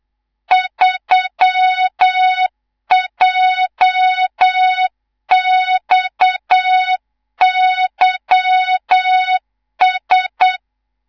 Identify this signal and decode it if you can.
Morse code, heard as a keyed tone, about 8 wpm, 3700 Hz, with noise. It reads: 3JXYS
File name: Morze-HA2MN / hang / 19 GYAKORLAT